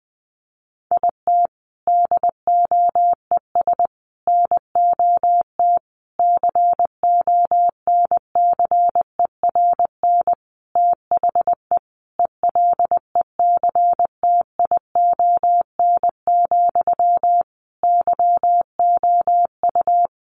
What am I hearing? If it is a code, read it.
Morse code, heard as a keyed tone, about 20 wpm, 700 Hz. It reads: IT DOES NOT CONCERN THE ELECTION, YOU
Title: Sentences from Top 1500 Words 20wpm